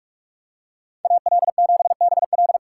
Morse code, heard as a keyed tone, about 46 wpm, 700 Hz. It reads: AL7BL
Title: USA Call Signs - 5 Characters - 3x Character Spacing - 45wpm